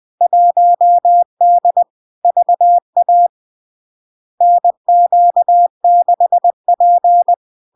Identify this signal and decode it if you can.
Morse code, heard as a keyed tone, about 20 wpm, 700 Hz. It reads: 1D VA NQ6P
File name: LARK-FD-2022-Log-in-CW-20wpm-700-Hz